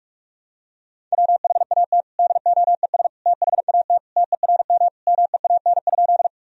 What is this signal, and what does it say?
Morse code, heard as a keyed tone, about 41 wpm, 700 Hz. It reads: WHAT DOES THAT TERM MEAN?
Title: Sentences from Top 1000 Words - Review Part 4 - 45wpm